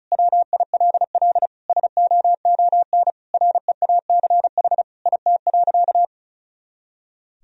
Morse code, heard as a keyed tone, about 36 wpm, 700 Hz. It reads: WILL SOON REACH IT.
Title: Sentences from Top 400 Words and Abbreviations 35wpm